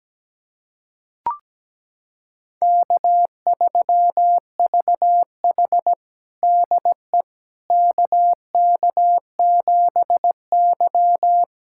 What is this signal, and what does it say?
Morse code, heard as a keyed tone, about 17 wpm, 700 Hz. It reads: K3VH DE KK7Y
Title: USA Callsign DE Callsign - 4 Characters - 17wpm